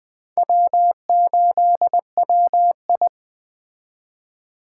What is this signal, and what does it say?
Morse code, heard as a keyed tone, about 21 wpm, 700 Hz. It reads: W8WI
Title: Speed Racing - USA 1x2 Call Signs 17wpm